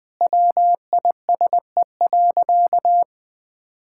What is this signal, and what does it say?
Morse code, heard as a keyed tone, about 20 wpm, 700 Hz. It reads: WISE.
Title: Sentences from Top 1600 Words 20wpm